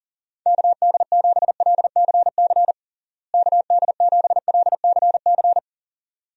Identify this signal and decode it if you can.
Morse code, heard as a keyed tone, about 37 wpm, 700 Hz. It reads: KD7LCC KD7LCC
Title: USA Call Signs - 6 Characters - Repeated 2x - 40wpm